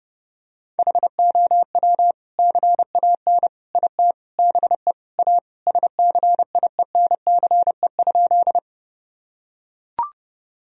Morse code, heard as a keyed tone, about 30 wpm, 700 Hz. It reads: HOW CAN IT BE A SCIENCE?
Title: Sentences from Top 1000 Words - Review Part 2 - 30wpm